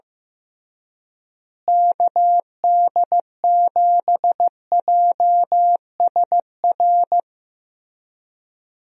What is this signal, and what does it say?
Morse code, heard as a keyed tone, about 15 wpm, 700 Hz. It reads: KD7JSR
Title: USA Call Signs - 6 Characters - Repeated 3x - 15wpm